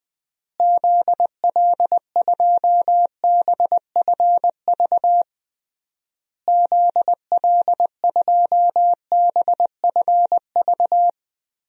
Morse code, heard as a keyed tone, about 20 wpm, 700 Hz. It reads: ZL2BFV ZL2BFV
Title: DXCC-100 Call Signs - 6 Characters - Repeated 3x - 20wpm